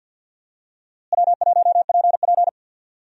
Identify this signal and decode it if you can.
Morse code, heard as a keyed tone, about 50 wpm, 700 Hz. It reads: W1PP
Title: USA Call Signs - 4 Characters - Repeated 2x - 50wpm